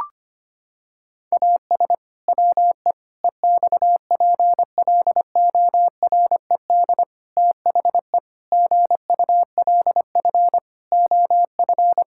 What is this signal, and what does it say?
Morse code, heard as a keyed tone, about 25 wpm, 700 Hz. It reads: AS WE EXPLORED THE GULF OF
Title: Pangram Challenge v2 - 25wpm